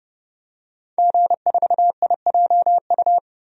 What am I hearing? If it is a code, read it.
Morse code, heard as a keyed tone, about 32 wpm, 700 Hz. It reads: G4IJU
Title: DXCC-100 Call Signs - 5 Characters - 30wpm